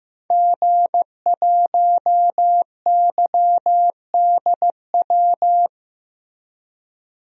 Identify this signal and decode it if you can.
Morse code, heard as a keyed tone, about 15 wpm, 700 Hz. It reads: G1YDW
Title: DXCC-100 Call Signs - 5 Characters - Repeated 2x - 15wpm